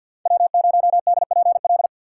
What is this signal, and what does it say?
Morse code, heard as a keyed tone, about 48 wpm, 700 Hz. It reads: W0DPL
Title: USA Call Signs - 5 Characters - 2x Character Spacing - 50wpm